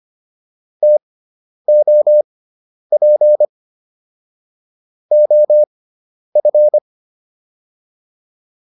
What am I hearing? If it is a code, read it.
Morse code, heard as a keyed tone, about 25 wpm, spaced as for 10 wpm, 600 Hz. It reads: TOP OF